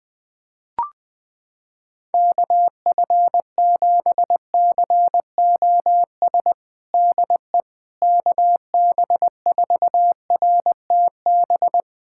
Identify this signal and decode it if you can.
Morse code, heard as a keyed tone, about 21 wpm, 700 Hz. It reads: KF7COS DE KB4RTB
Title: USA Callsign DE Callsign - 6 Characters - 20wpm